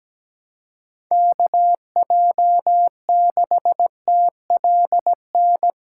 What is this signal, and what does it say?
Morse code, heard as a keyed tone, about 17 wpm, 700 Hz. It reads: KJ6TLN
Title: USA Callsign DE Callsign - 6 Characters - 17wpm